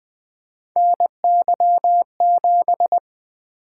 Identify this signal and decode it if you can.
Morse code, heard as a keyed tone, about 20 wpm, 700 Hz. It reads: NY7
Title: Three Letter-Number Combinations - 20wpm